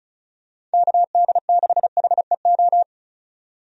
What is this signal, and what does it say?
Morse code, heard as a keyed tone, about 34 wpm, 700 Hz. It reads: KD6HEO
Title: USA Call Signs - 6 Characters - 35wpm